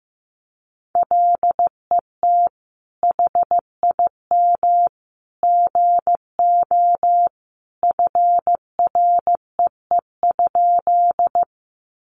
Morse code, heard as a keyed tone, about 15 wpm, 700 Hz. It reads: LET HIM GO FREE?
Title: Sentences from Top 500 Words 15wpm